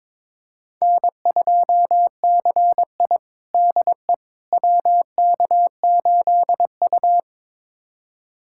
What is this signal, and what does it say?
Morse code, heard as a keyed tone, about 21 wpm, 700 Hz. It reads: N2CI DE WK8U
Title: USA Callsign DE Callsign - 4 Characters - 22wpm